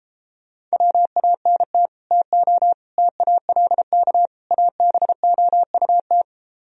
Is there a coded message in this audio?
Yes: Morse code from 0.7 to 6.2 s, heard as a keyed tone, about 32 wpm, 700 Hz. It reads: WANT TO TALK ABOUT